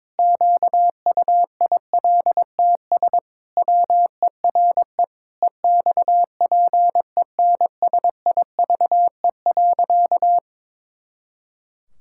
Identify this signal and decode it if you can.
Morse code, heard as a keyed tone, about 22 wpm, 700 Hz. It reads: QUILTS WERE EXPENSIVE.
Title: Pangram Challenge v2 - 22wpm